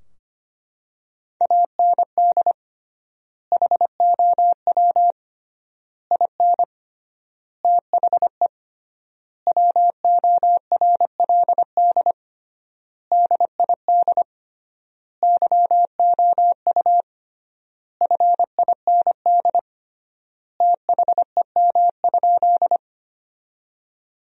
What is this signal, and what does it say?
Morse code, heard as a keyed tone, about 25 wpm, 700 Hz. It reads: AND HOW IN THE WORLD DID YOU FIND THEM?
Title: 3x Word-Spacing - Sentences from Top 200 Words - 25wpm